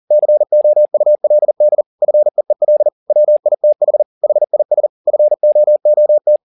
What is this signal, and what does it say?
Morse code, heard as a keyed tone, about 40 wpm, 600 Hz. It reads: COULD FEEL WITH HIS FOOT
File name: the old man and the sea 2 40wpm